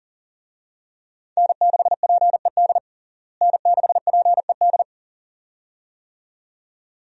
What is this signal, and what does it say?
Morse code, heard as a keyed tone, about 40 wpm, 700 Hz. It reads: N6PED N6PED
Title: USA Call Signs - 5 Characters - Repeated 2x - 40wpm